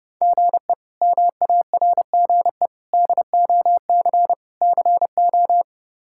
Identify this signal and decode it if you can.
Morse code, heard as a keyed tone, about 30 wpm, 700 Hz. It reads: GE MARGE DOC CO
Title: K1USN SST QSO Practice - 30wpm